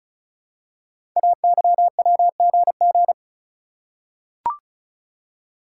Morse code, heard as a keyed tone, about 36 wpm, 690 Hz. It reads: AYWGG
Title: Code Groups - 5 Letters - 35wpm